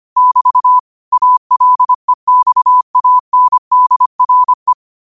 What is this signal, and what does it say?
Morse code, heard as a keyed tone, about 25 wpm, 1000 Hz. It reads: <BT> ALEXANDRE
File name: qotdfiles / 25wpm